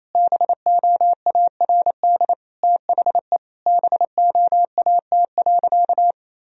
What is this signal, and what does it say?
Morse code, heard as a keyed tone, about 28 wpm, 700 Hz. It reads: BOARD THE BOAT.
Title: Sentences from Top 1000 Words - Review Part 4 - 28wpm